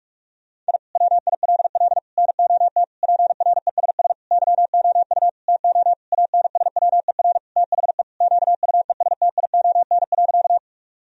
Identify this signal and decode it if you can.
Morse code, heard as a keyed tone, about 46 wpm, 700 Hz. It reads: I WILL NOT PRESS YOU TO ANSWER THE QUESTION.
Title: Sentences from Top 1000 Words - Review Part 1 - 45wpm